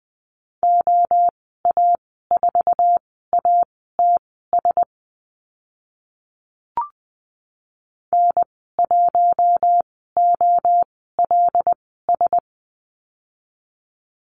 Morse code, heard as a keyed tone, about 20 wpm, spaced as for 14 wpm, 700 Hz. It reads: OA4ATS N1OLS
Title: 076 Lesson 13g - Letter L - Call Signs 20wpm